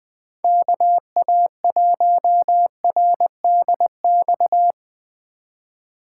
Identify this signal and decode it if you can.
Morse code, heard as a keyed tone, about 19 wpm, 700 Hz. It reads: KA1RDX
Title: Speed Racing - USA 2x3 Call Signs 17wpm